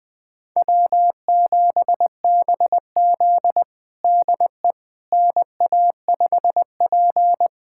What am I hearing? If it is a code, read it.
Morse code, heard as a keyed tone, about 21 wpm, 700 Hz. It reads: W7BZ DE NA5P
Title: USA Callsign DE Callsign - 4 Characters - 20wpm